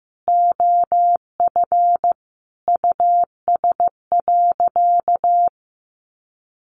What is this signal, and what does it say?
Morse code, heard as a keyed tone, about 15 wpm, 700 Hz. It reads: OF US.
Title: Sentences from Top 300 Words 15wpm